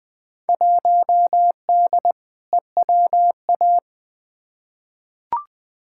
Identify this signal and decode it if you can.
Morse code, heard as a keyed tone, about 20 wpm, 700 Hz. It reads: 1D EWA
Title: Field Day Exchanges - 20wpm